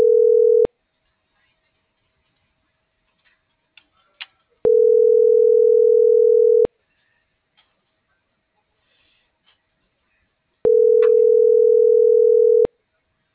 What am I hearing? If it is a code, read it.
Ringback tone (North America), 440 + 480 Hz.